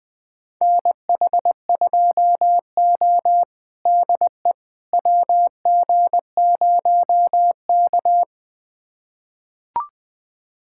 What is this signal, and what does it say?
Morse code, heard as a keyed tone, about 21 wpm, 700 Hz. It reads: NH2O DE WG0K
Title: USA Callsign DE Callsign - 4 Characters - 20wpm